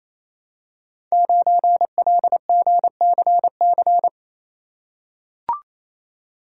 Morse code, heard as a keyed tone, about 27 wpm, 700 Hz. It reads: 9LGCC
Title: Code Groups - 5 Letter:Numbers - 28wpm